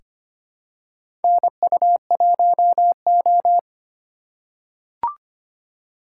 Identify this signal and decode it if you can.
Morse code, heard as a keyed tone, about 25 wpm, 700 Hz. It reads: NU1O
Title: K1USN SST QSO Practice - 25wpm